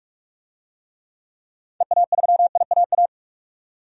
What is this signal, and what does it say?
Morse code, heard as a keyed tone, about 46 wpm, 700 Hz. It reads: EA3IAA
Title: DXCC-100 Call Signs - 6 Characters - 45wpm